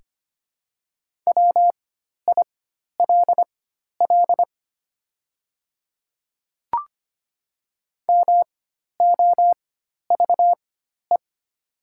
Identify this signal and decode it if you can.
Morse code, heard as a keyed tone, about 25 wpm, spaced as for 12 wpm, 700 Hz. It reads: WILL MOVE
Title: 4x Character Spacing - Top 300 Words - 25wpm